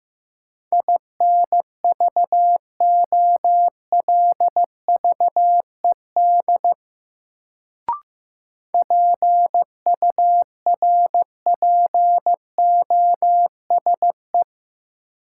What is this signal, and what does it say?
Morse code, heard as a keyed tone, about 15 wpm, 700 Hz. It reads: INVOLVED PURPOSE
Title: Sentences from Top 800 Words - New Word Drill - 15wpm